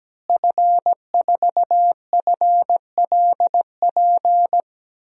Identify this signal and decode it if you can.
Morse code, heard as a keyed tone, about 17 wpm, 700 Hz. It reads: F4FLP
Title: DXCC-100 Call Signs - 5 Characters - Repeated 3x - 17wpm